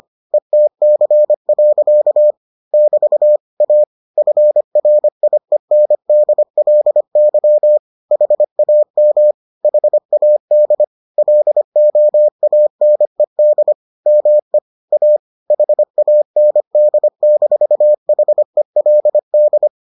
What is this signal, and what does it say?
Morse code, heard as a keyed tone, about 25 wpm, 600 Hz. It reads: ETC. <BT> A FRIENDLY HAM HAD LOANED ME A HAND-HELD